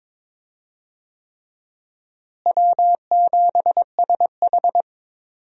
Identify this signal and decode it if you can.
Morse code, heard as a keyed tone, about 23 wpm, 700 Hz. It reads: W7SH
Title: USA Call Signs - 4 Characters - Repeated 3x - 22wpm